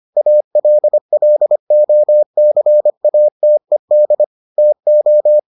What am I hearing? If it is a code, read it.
Morse code, heard as a keyed tone, about 25 wpm, 600 Hz. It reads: ALLOCATED TO